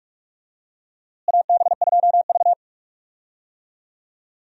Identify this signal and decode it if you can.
Morse code, heard as a keyed tone, about 42 wpm, 700 Hz. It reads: AB2V
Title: USA Call Signs - 4 Characters - Repeated 2x - 45wpm